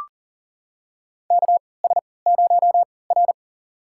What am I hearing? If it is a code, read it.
Morse code, heard as a keyed tone, about 40 wpm, spaced as for 23 wpm, 700 Hz. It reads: KS0R
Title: USA Call Signs - 4 Characters - 3x Character Spacing - 40wpm